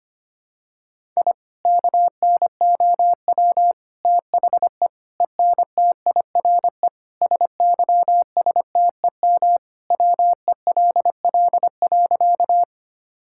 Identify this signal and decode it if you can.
Morse code, heard as a keyed tone, about 25 wpm, 700 Hz. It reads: I KNOW THE ENTIRE SYSTEM WELL.
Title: Sentences from Top 1000 Words - Review Part 5 - 25wpm